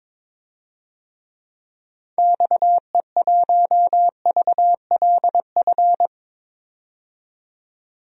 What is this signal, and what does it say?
Morse code, heard as a keyed tone, about 21 wpm, 700 Hz. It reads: XE1VLF
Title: DXCC-100 Call Signs - 6 Characters - 22wpm